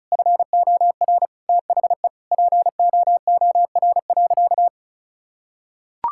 Morse code, heard as a keyed tone, about 36 wpm, 700 Hz. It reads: FOR THE POOR.
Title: Sentences from Top 1000 Words - Review Part 6 - 35wpm